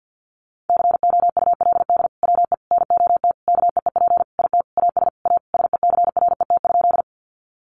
Morse code, heard as a keyed tone, about 50 wpm, 700 Hz. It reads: COULD WE NOT KEEP IT AS A SECRET?
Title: Sentences from Top 1200 Words 50wpm